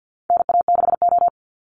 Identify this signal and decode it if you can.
Morse code, heard as a keyed tone, about 50 wpm, 700 Hz. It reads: NA6O
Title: Call Signs Repeated 2x - 50wpm